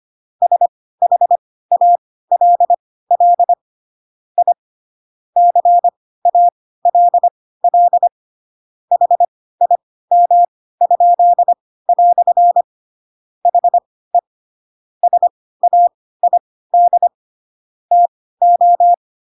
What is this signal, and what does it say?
Morse code, heard as a keyed tone, about 25 wpm, spaced as for 16 wpm, 700 Hz. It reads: SHALL I CALL HIM?" HE SAID TO
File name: Pinocchio / Pinocchio ch3 10 25 ik1hge